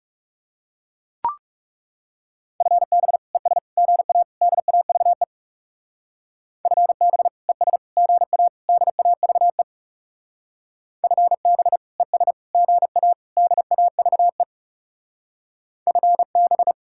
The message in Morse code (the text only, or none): FB ES GA DAVE FB ES GA DAVE FB ES GA DAVE FB